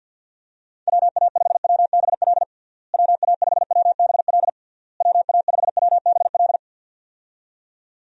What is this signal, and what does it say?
Morse code, heard as a keyed tone, about 52 wpm, 700 Hz. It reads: WA5WBL WA5WBL WA5WBL
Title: USA Call Signs - 6 Characters - Repeated 3x - 50wpm